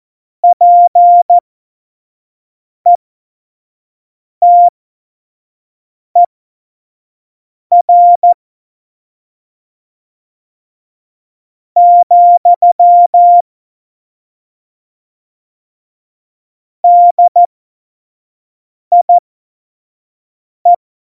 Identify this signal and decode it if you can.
Morse code, heard as a keyed tone, about 14 wpm, spaced as for 4 wpm, 700 Hz. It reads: PETER, DIE
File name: Morsekurs / Witze / Witz 23. Juli 2019, 25 F70